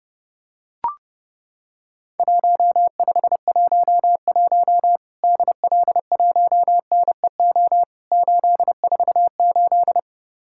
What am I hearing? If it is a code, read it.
Morse code, heard as a keyed tone, about 30 wpm, 700 Hz. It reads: 1511 DL1NEO 848
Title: Worked All Europe DX Contest - QTC Exchange - 30wpm